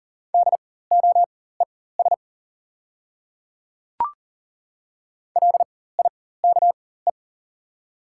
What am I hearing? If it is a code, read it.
Morse code, heard as a keyed tone, about 38 wpm, spaced as for 18 wpm, 690 Hz. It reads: DOES LIKE
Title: 4x Character Spacing - Top 500 Words - 40wpm